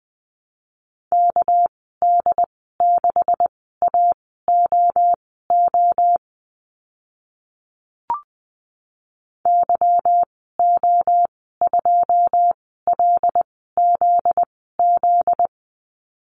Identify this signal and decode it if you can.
Morse code, heard as a keyed tone, about 20 wpm, spaced as for 14 wpm, 700 Hz. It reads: KD6AOO YO2LZZ
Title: 245 Lesson 39g - Letter z - Call Signs 20wpm